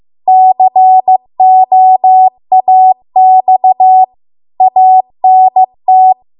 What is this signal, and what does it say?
Morse code, heard as a keyed tone, about 15 wpm, 750 Hz. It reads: COAX ANT
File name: Archive / 15 WPM / 15WPM